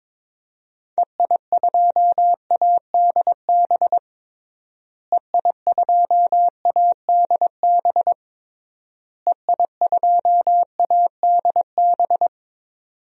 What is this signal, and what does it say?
Morse code, heard as a keyed tone, about 22 wpm, 700 Hz. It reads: EI2ADB EI2ADB EI2ADB
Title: DXCC-100 Call Signs - 6 Characters - Repeated 3x - 22wpm